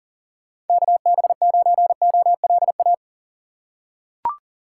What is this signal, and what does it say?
Morse code, heard as a keyed tone, about 40 wpm, 700 Hz. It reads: KB9OLA
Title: USA Call Signs - 6 Characters - 3x Character Spacing - 40wpm